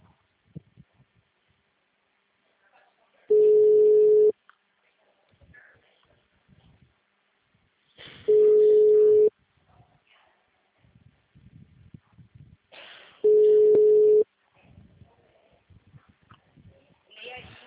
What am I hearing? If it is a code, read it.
Ringback tone (Europe, 425 Hz), with speech.